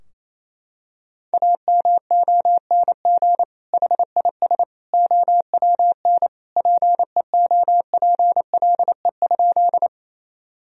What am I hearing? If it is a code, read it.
Morse code, heard as a keyed tone, about 28 wpm, 700 Hz. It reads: AMONG HIS OWN PEOPLE?
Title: Sentences from Top 1000 Words - Review Part 3 - 28wpm